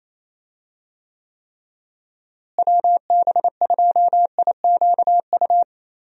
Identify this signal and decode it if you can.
Morse code, heard as a keyed tone, about 28 wpm, 700 Hz. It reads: WB2IQU
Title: USA Call Signs - 6 Characters - Repeated 3x - 28wpm